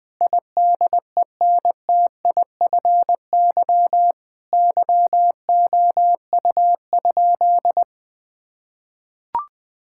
Morse code, heard as a keyed tone, about 21 wpm, 700 Hz. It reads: IDENTIFY YOU?
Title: Sentences from Top 1000 Words - Review Part 6 - 20wpm